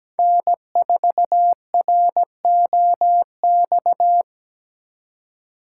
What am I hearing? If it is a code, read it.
Morse code, heard as a keyed tone, about 20 wpm, 700 Hz. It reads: N4ROX
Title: Speed Racing - USA 1x3 Call Signs 17wpm